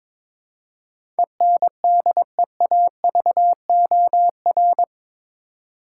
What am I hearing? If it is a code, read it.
Morse code, heard as a keyed tone, about 22 wpm, 700 Hz. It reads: ENDEAVOR
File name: practice / 22wpm / Eight Letter Words - 22wpm